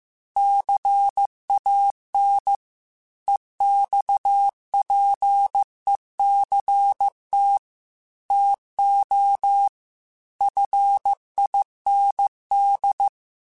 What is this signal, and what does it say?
Morse code, heard as a keyed tone, about 15 wpm, 800 Hz. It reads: CAN EXPECT TO FIND
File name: Technical Subjects / Technical Subjects-mp3s / Sw Pwr Sup @13